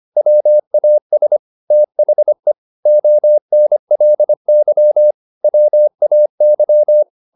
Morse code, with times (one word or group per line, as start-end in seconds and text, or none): 0.2-1.4 WAS
1.7-2.5 THE
2.9-5.1 ONLY
5.4-7.0 WAY